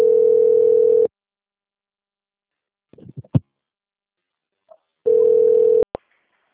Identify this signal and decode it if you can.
Ringback tone (North America), 440 + 480 Hz.
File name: out- - - - - .32262